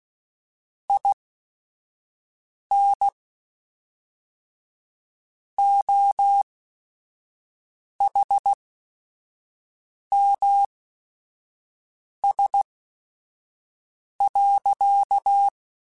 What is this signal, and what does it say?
Morse code, heard as a keyed tone, about 16 wpm, spaced as for 5 wpm, 800 Hz. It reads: IN OHMS.
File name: Technical Subjects / Technical Subjects-mp3s / XL @5